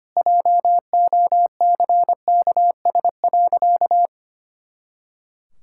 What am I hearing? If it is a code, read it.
Morse code, heard as a keyed tone, about 25 wpm, 700 Hz. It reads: JOCKS.
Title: Pangram Challenge v2 - 25wpm